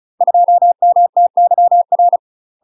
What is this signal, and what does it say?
Morse code, heard as a keyed tone, about 34 wpm, 700 Hz. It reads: 2MTYR
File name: UK MASTER QSOs TRAINING / 35WPM / UK MASTER QSOs 0042